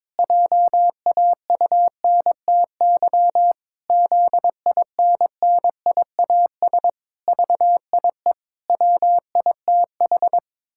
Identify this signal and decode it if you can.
Morse code, heard as a keyed tone, about 22 wpm, 700 Hz. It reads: JAUNTY ZINNIAS VIE WITH
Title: Pangram Challenge v2 - 22wpm